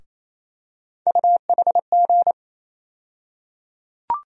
Morse code, heard as a keyed tone, about 28 wpm, 700 Hz. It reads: UHG
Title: USA Call Sign Postfixes - 3 Letters 28wpm